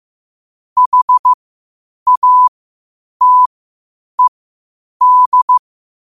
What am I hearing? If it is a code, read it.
Morse code, heard as a keyed tone, about 15 wpm, spaced as for 8 wpm, 1000 Hz. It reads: HATED
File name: qotdfiles / 8wpm